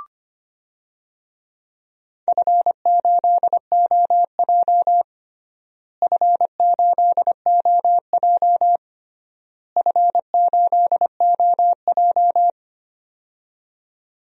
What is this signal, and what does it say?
Morse code, heard as a keyed tone, about 25 wpm, 700 Hz. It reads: F8OJ F8OJ F8OJ
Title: DXCC-100 Call Signs - 4 Characters - Repeated 3x - 25wpm